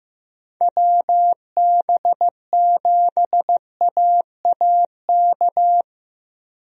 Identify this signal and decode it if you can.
Morse code, heard as a keyed tone, about 15 wpm, 700 Hz. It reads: WB7AAK
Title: USA Call Signs - 6 Characters - Repeated 2x - 15wpm